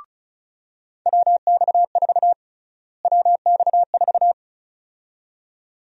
Morse code, heard as a keyed tone, about 36 wpm, 700 Hz. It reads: WX4 WX4
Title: USA Call Sign Prefixes - Repeated 2x - 35wpm